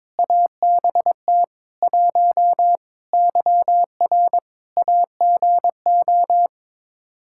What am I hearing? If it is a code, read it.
Morse code, heard as a keyed tone, about 22 wpm, 700 Hz. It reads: ABT 1 YR AGO
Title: Practice QSO - Exchange 3 - More Info - 22wpm